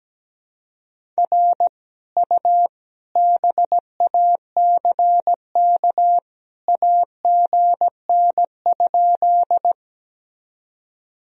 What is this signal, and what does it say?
Morse code, heard as a keyed tone, about 17 wpm, 700 Hz. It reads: R U BACK AGN?
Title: Sentences from Top 200 Words and Abbreviations 17wpm